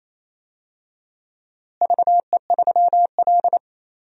Morse code, heard as a keyed tone, about 27 wpm, 700 Hz. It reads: VE3L
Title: DXCC-100 Call Signs - 4 Characters - Repeated 3x - 28wpm